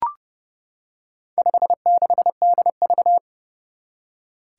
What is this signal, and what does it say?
Morse code, heard as a keyed tone, about 30 wpm, 700 Hz. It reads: 56DV
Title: Four Challenging Letter-Number Combinations 30wpm